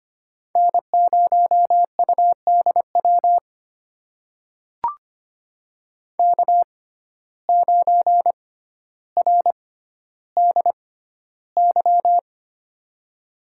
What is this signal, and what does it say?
Morse code, heard as a keyed tone, about 25 wpm, spaced as for 13 wpm, 700 Hz. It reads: N0UDW K9RDY
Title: USA Call Signs - 5 Characters - 6x Character Spacing - 25wpm